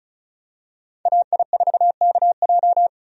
Morse code, heard as a keyed tone, about 34 wpm, 700 Hz. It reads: AI4KJ
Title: USA Call Signs - 5 Characters - 35wpm